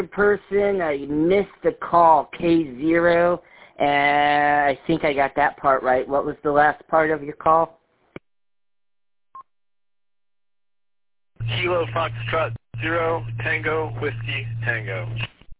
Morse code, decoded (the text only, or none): none